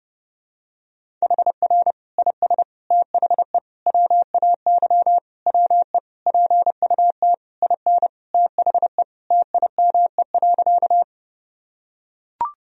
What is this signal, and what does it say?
Morse code, heard as a keyed tone, about 30 wpm, 700 Hz. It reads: HR IS THE WAY WE PUT IN THE TIME.
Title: Sentences from Top 200 Words and Abbreviations 30wpm